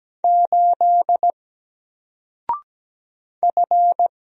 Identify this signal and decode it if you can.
Morse code, heard as a keyed tone, about 17 wpm, 700 Hz. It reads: 8 F
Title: Single Character (Letter:Number:Punctuation:Prosign) Alt 17wpm